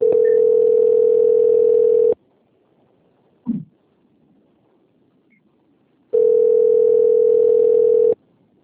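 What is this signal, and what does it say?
Ringback tone (North America), 440 + 480 Hz.
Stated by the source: Cornelio Procopio - PR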